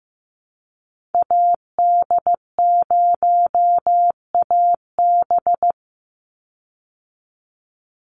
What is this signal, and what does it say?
Morse code, heard as a keyed tone, about 15 wpm, 700 Hz. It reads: AD0AB
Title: Call Signs Repeated 3x - 15wpm